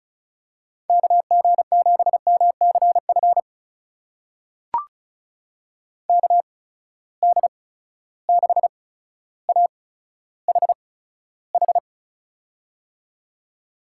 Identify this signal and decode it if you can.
Morse code, heard as a keyed tone, about 36 wpm, spaced as for 15 wpm, 700 Hz. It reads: KG7MCF KD6AHH
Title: USA Call Signs - 6 Characters - 8x Character Spacing - 35wpm